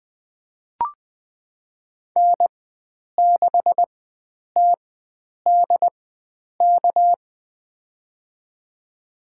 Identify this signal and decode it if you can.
Morse code, heard as a keyed tone, about 20 wpm, spaced as for 9 wpm, 700 Hz. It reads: N6TDK
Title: USA Call Signs - 5 Characters - 4x Character Spacing - 20wpm